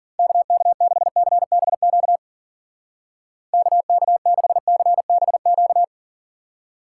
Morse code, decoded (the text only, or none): KK6CBQ KK6CBQ